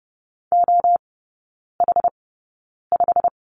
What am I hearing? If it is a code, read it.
Morse code, heard as a keyed tone, about 30 wpm, spaced as for 9 wpm, 700 Hz. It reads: OH5
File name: course / 30:10wpm / 087 Lesson 15e - Number 5 - Three Characters 30wpm